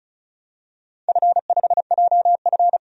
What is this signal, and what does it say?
Morse code, heard as a keyed tone, about 34 wpm, 700 Hz. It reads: F5JF
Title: DXCC-100 Call Signs - 4 Characters - 35wpm